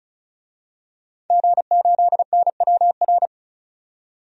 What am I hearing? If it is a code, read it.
Morse code, heard as a keyed tone, about 34 wpm, 700 Hz. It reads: G8NWR
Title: DXCC-100 Call Signs - 5 Characters - 35wpm